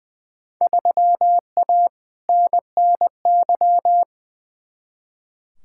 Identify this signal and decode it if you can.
Morse code, heard as a keyed tone, about 21 wpm, 700 Hz. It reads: 3A NNY
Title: Field Day Exchanges - 20wpm